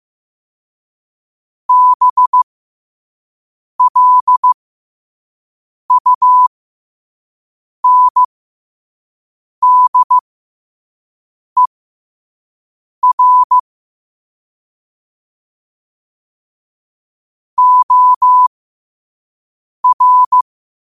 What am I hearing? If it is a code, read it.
Morse code, heard as a keyed tone, about 15 wpm, spaced as for 5 wpm, 1000 Hz. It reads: BLUNDER OR
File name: qotdfiles / 5wpm